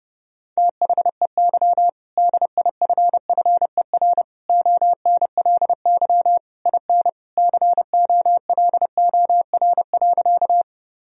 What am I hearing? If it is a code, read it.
Morse code, heard as a keyed tone, about 30 wpm, 700 Hz. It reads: THEY DIFFER ONLY IN COLOR.
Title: Sentences from Top 1000 Words - Review Part 2 - 30wpm